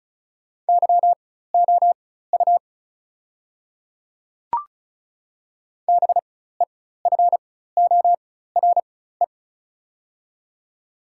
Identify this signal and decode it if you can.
Morse code, heard as a keyed tone, about 36 wpm, spaced as for 16 wpm, 700 Hz. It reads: YOU BEFORE
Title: 4x Character Spacing - Top 200 Words - 35wpm